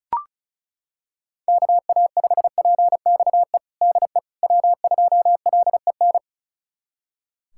Morse code, heard as a keyed tone, about 36 wpm, 700 Hz. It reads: KA5PXE DE W2LEN
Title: USA Callsign DE Callsign 35wpm